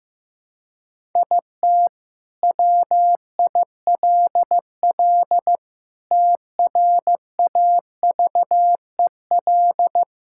Morse code, heard as a keyed tone, about 15 wpm, 700 Hz. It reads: IT WILL TRAVEL
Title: Sentences from Top 1000 Words - Review Part 6 - 15wpm